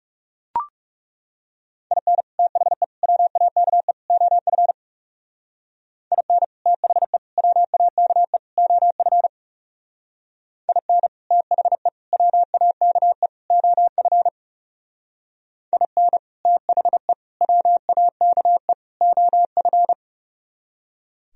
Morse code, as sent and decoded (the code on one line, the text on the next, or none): .. -. / - .... / .-- .- -.- . / --- ..-. / .. -. / - .... / .-- .- -.- . / --- ..-. / .. -. / - .... / .-- .- -.- . / --- ..-. / .. -. / - .... / .-- .- -.- . / --- ..-.
IN THE WAKE OF IN THE WAKE OF IN THE WAKE OF IN THE WAKE OF